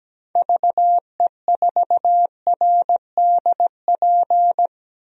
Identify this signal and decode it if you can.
Morse code, heard as a keyed tone, about 17 wpm, 700 Hz. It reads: VE4RDP
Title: DXCC-100 Call Signs - 6 Characters - 17wpm